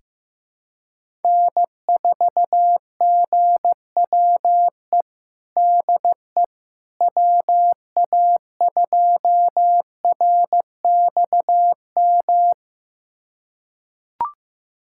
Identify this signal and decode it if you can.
Morse code, heard as a keyed tone, about 15 wpm, 700 Hz. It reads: N4GWE DE WA2RXM
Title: USA Callsign DE Callsign 15wpm